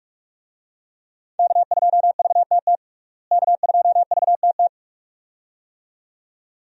Morse code, heard as a keyed tone, about 46 wpm, 700 Hz. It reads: K2VTT K2VTT
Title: USA Call Signs - 5 Characters - Repeated 2x - 45wpm